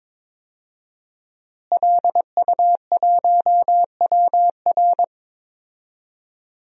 Morse code, heard as a keyed tone, about 21 wpm, 700 Hz. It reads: LU1WR
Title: DXCC-100 Call Signs - 5 Characters - Repeated 3x - 22wpm